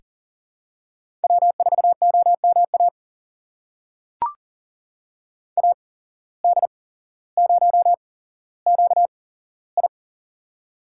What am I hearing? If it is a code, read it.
Morse code, heard as a keyed tone, about 40 wpm, spaced as for 17 wpm, 700 Hz. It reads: W4OMA AD0QI
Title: USA Call Signs - 8x Character Spacing - 40wpm